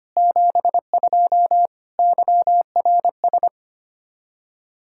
Morse code, heard as a keyed tone, about 25 wpm, 700 Hz. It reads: 72 YRS
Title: K1USN SST QSO Practice 25wpm